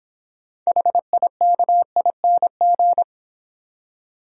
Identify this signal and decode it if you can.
Morse code, heard as a keyed tone, about 26 wpm, 700 Hz. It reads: HIKING